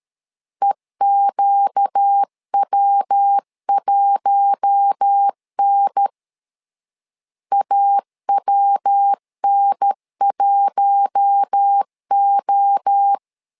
Morse code, heard as a keyed tone, about 13 wpm, 800 Hz. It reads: EQW1N AWN1O
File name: LEZIONI / 4 Q1 12wpm 2028